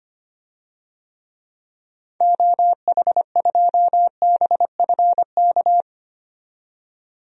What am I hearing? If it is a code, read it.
Morse code, heard as a keyed tone, about 25 wpm, 700 Hz. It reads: OH2BFK
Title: DXCC-100 Call Signs - 6 Characters - 25wpm